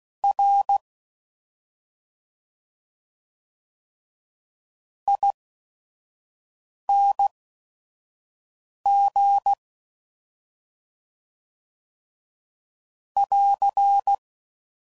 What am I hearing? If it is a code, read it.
Morse code, heard as a keyed tone, about 16 wpm, spaced as for 5 wpm, 800 Hz. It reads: R ING <AR>
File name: Suffixes / Special ING's / Sngl ING's